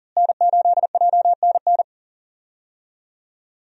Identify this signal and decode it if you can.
Morse code, heard as a keyed tone, about 40 wpm, 700 Hz. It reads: N8JNN
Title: USA Call Signs - 5 Characters - Repeated 2x - 40wpm